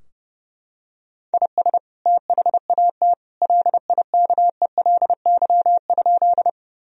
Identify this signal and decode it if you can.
Morse code, heard as a keyed tone, about 30 wpm, 700 Hz. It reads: IS THAT LIKELY?
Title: Sentences from Top 1000 Words - Review Part 4 - 30wpm